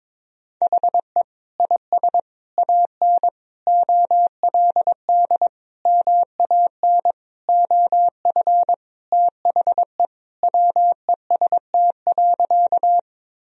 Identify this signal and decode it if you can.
Morse code, heard as a keyed tone, about 21 wpm, 700 Hz. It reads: HE IS AN OLD MAN OF THE WEST.
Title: Sentences from Top 1000 Words - Review Part 2 - 22wpm